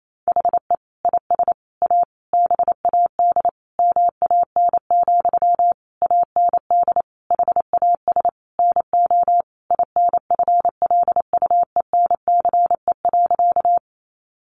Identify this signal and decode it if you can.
Morse code, heard as a keyed tone, about 28 wpm, 700 Hz. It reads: HE IS A BAD MAN, AND HAS NO INFLUENCE.
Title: Sentences from Top 1100 Words 28wpm